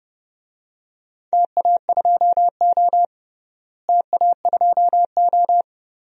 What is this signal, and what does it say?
Morse code, heard as a keyed tone, about 30 wpm, 700 Hz. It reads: TA2O TA2O
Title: DXCC-100 Call Signs - 4 Characters - Repeated 3x - 30wpm